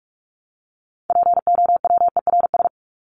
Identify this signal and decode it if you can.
Morse code, heard as a keyed tone, about 44 wpm, 700 Hz. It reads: POWERS
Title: ES:S Suffix - Top 100 Words with Base Words 45wpm